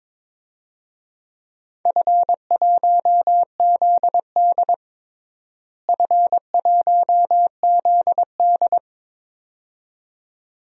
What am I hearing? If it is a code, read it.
Morse code, heard as a keyed tone, about 22 wpm, 700 Hz. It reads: F1ZD F1ZD
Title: DXCC-100 Call Signs - 4 Characters - Repeated 2x - 22wpm